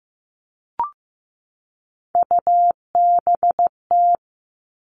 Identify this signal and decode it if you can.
Morse code, heard as a keyed tone, about 15 wpm, 700 Hz. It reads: UBT
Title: All Cut Numbers - 3-Digit 15wpm